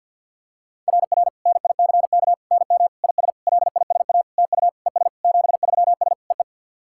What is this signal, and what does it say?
Morse code, heard as a keyed tone, about 51 wpm, 690 Hz. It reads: RR NICK NM IS LISA TU ES 73S EE
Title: SOTA QSO Practice 50wpm